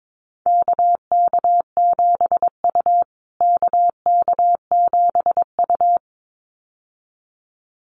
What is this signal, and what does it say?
Morse code, heard as a keyed tone, about 22 wpm, 700 Hz. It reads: KK7U KK7U
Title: USA Call Signs Repeated 2x 22wpm